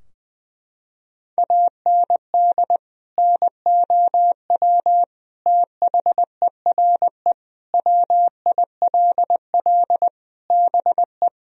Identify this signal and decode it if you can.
Morse code, heard as a keyed tone, about 21 wpm, 700 Hz. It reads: AND NOW THERE WILL BE
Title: Sentences from Top 1000 Words - Review Part 2 - 20wpm